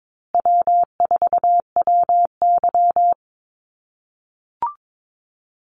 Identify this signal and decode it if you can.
Morse code, heard as a keyed tone, about 22 wpm, 700 Hz. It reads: W4WY
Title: USA Call Signs Repeated 3x 22wpm